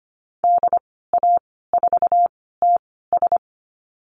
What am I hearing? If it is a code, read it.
Morse code, heard as a keyed tone, about 25 wpm, spaced as for 16 wpm, 700 Hz. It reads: DA4TS
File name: Lesson 15g - Number 5 - Call Signs 25wpm